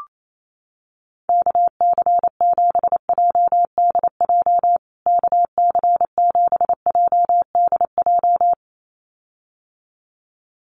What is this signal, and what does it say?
Morse code, heard as a keyed tone, about 28 wpm, 700 Hz. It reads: KC7JDJ KC7JDJ
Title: USA Call Signs Repeated 2x 28wpm